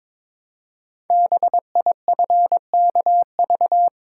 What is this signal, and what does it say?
Morse code, heard as a keyed tone, about 22 wpm, 700 Hz. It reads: BIFKV
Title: Code Groups - 5 Letter:Numbers - 22wpm